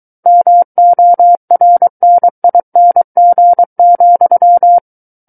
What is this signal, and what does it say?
Morse code, heard as a keyed tone, about 23 wpm, 700 Hz. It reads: MORNING,
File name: Book One / 14-Chapter 14 - 23 WPM